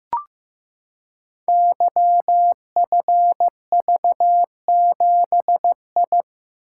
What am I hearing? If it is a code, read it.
Morse code, heard as a keyed tone, about 15 wpm, 700 Hz. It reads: YFV7I
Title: Code Groups - 5 Letter:Numbers - 15wpm